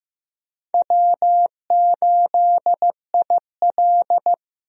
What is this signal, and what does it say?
Morse code, heard as a keyed tone, about 15 wpm, 700 Hz. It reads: W8IL
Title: USA Call Signs - 4 Characters - Repeated 2x - 15wpm